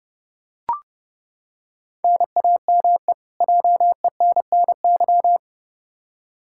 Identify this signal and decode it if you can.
Morse code, heard as a keyed tone, about 30 wpm, 700 Hz. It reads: NAME JENNY
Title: K1USN SST QSO Practice 30wpm